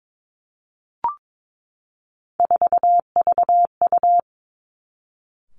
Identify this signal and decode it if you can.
Morse code, heard as a keyed tone, about 22 wpm, 700 Hz. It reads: 4VU
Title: Three Challenging Letter-Number Combinations 22wpm